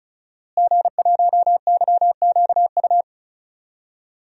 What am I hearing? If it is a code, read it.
Morse code, heard as a keyed tone, about 36 wpm, 700 Hz. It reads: G1YQU
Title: DXCC-100 Call Signs - 5 Characters - Repeated 2x - 35wpm